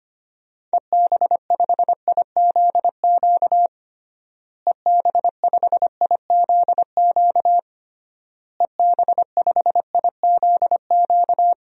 Morse code, heard as a keyed tone, about 25 wpm, 700 Hz. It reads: EB5IZQ EB5IZQ EB5IZQ
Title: DXCC-100 Call Signs - 6 Characters - Repeated 3x - 25wpm